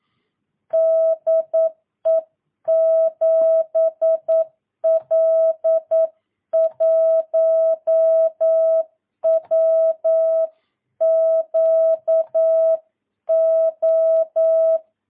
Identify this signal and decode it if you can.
Morse code, heard as a keyed tone, about 9 wpm, 650 Hz. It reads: DE 7L1WQO
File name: jh8cbh / 2022cw